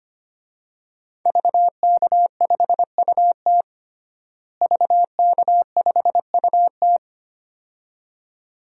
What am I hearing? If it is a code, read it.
Morse code, heard as a keyed tone, about 25 wpm, 700 Hz. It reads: VK5UT VK5UT
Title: DXCC-100 Call Signs - 5 Characters - Repeated 2x - 25wpm